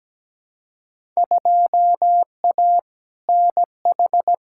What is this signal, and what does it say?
Morse code, heard as a keyed tone, about 17 wpm, 700 Hz. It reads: 2A NH
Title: Field Day Exchanges - 17wpm